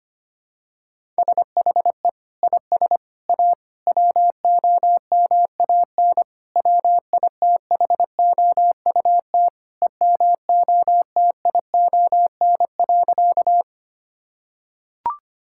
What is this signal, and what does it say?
Morse code, heard as a keyed tone, about 25 wpm, 700 Hz. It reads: SHE IS A WOMAN WITHOUT EMOTION.
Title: Sentences from Top 1600 Words 25wpm